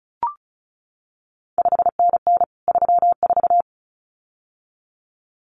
Morse code, heard as a keyed tone, about 36 wpm, 700 Hz. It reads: 5NN 34
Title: CQ WW DX Contest Exchange Encore - 35wpm